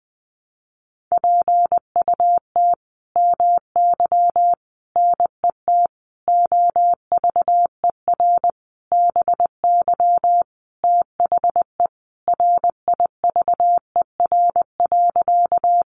Morse code, heard as a keyed tone, about 20 wpm, 700 Hz. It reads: PUT MY NET OVER BY THE RIVER.
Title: Sentences from Top 1400 Words 20wpm